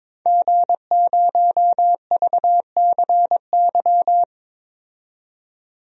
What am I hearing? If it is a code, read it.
Morse code, heard as a keyed tone, about 22 wpm, 700 Hz. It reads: G0VCY
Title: DXCC-100 Call Signs - 5 Characters - 22wpm